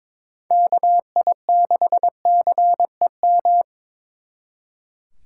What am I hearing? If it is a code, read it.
Morse code, heard as a keyed tone, about 26 wpm, 700 Hz. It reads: KI6CEM
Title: Speed Racing - USA 2x3 Call Signs 22wpm